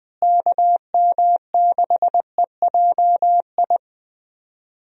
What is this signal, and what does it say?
Morse code, heard as a keyed tone, about 21 wpm, 700 Hz. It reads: KM6EJI
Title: USA Call Signs - 6 Characters - Repeated 3x - 20wpm